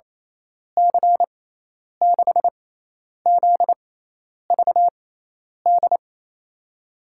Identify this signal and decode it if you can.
Morse code, heard as a keyed tone, about 29 wpm, spaced as for 10 wpm, 700 Hz. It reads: C6ZVD
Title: USA Call Signs - 6 Characters - 6x Character Spacing - 28wpm